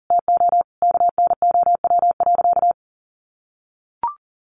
Morse code, heard as a keyed tone, about 40 wpm, 700 Hz. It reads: TO KNOW.
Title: Sentences from Top 1100 Words 40wpm